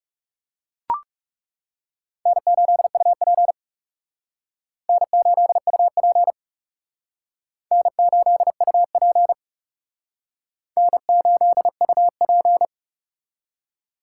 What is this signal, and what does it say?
Morse code, heard as a keyed tone, about 37 wpm, 700 Hz. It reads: N8UP N8UP N8UP N8UP
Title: Speed Racing - USA Call Signs 30wpm